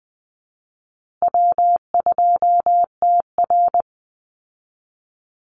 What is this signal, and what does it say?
Morse code, heard as a keyed tone, about 20 wpm, 700 Hz. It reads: W2TR
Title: Easy Call Signs Repeated 3x - 20wpm